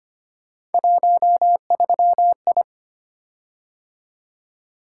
Morse code, heard as a keyed tone, about 25 wpm, 700 Hz. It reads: 13I
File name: Three Letter-Number Combinations - 25wpm